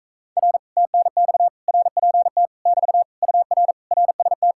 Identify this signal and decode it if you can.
Morse code, heard as a keyed tone, about 45 wpm, 700 Hz. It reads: R TNX RPT <BT> UR RST